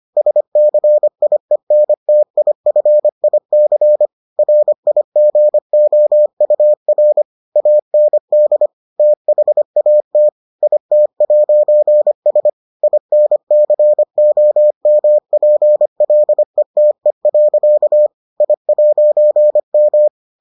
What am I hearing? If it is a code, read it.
Morse code, heard as a keyed tone, about 25 wpm, 600 Hz. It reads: SCIENTIFIC RIGOUR AND THAT IT'S INCOMPLETE. I'M